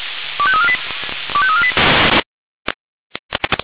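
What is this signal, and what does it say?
Five-tone selective call (ZVEI), heard as quick runs of five tones, tones of about 70 ms. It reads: 25348 25348
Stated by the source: ZVEI five-tone call 25344 received by SDR.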